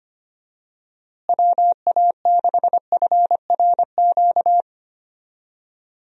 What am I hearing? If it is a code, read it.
Morse code, heard as a keyed tone, about 25 wpm, 700 Hz. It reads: WA6FRQ
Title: USA Call Signs - 6 Characters - Repeated 2x - 25wpm